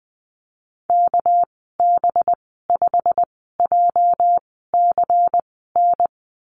Morse code, heard as a keyed tone, about 20 wpm, spaced as for 14 wpm, 700 Hz. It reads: KB5JCN
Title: 231 Lesson 37g - Letter x - Call Signs 20wpm